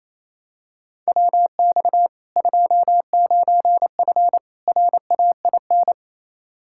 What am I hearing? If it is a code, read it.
Morse code, heard as a keyed tone, about 28 wpm, 700 Hz. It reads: WX 29F RAIN
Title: K1USN SST QSO Practice 28wpm